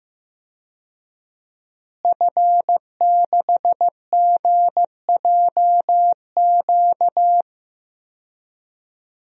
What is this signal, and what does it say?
Morse code, heard as a keyed tone, about 15 wpm, 700 Hz. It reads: F6GJQ
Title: DXCC-100 Call Signs - 5 Characters - Repeated 2x - 15wpm